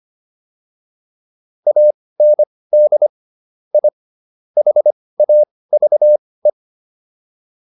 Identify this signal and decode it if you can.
Morse code, heard as a keyed tone, about 25 wpm, spaced as for 18 wpm, 600 Hz. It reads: AND I HAVE